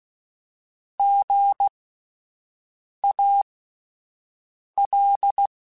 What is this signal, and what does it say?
Morse code, heard as a keyed tone, about 16 wpm, spaced as for 5 wpm, 800 Hz. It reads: GAL
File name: Alphabetical / Alphabetical mp3s / Gs @5